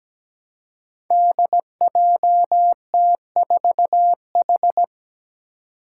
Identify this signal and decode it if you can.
Morse code, heard as a keyed tone, about 17 wpm, 700 Hz. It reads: DJT4H
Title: Five Challenging Letter-Number Combinations 17wpm